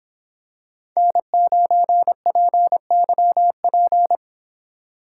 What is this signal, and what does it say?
Morse code, heard as a keyed tone, about 26 wpm, 700 Hz. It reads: N9PYP